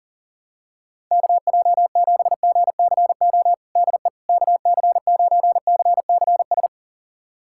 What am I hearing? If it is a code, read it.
Morse code, heard as a keyed tone, about 40 wpm, 700 Hz. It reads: KJ7GCO DE KC9CCS
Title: USA Callsign DE Callsign 40wpm